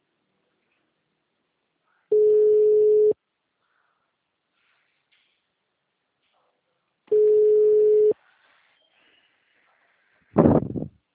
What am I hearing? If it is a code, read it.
Ringback tone (Europe, 425 Hz).